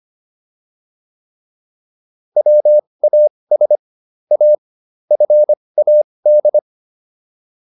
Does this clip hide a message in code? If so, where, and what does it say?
Yes: Morse code from 2.4 to 6.6 s, heard as a keyed tone, about 25 wpm, spaced as for 20 wpm, 600 Hz. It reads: WAS A FAD